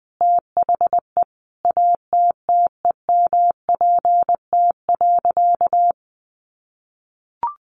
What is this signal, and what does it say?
Morse code, heard as a keyed tone, about 20 wpm, 700 Hz. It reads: THE ATTEMPT.
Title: Sentences from Top 1000 Words 20wpm